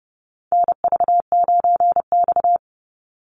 Morse code, heard as a keyed tone, about 30 wpm, 700 Hz. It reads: NV9X
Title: Call Signs Repeated 2x - 30wpm